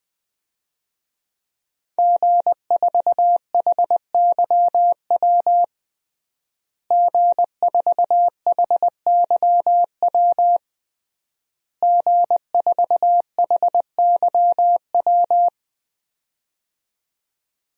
Morse code, heard as a keyed tone, about 21 wpm, 700 Hz. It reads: G4HYW G4HYW G4HYW
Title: DXCC-100 Call Signs - 5 Characters - Repeated 3x - 20wpm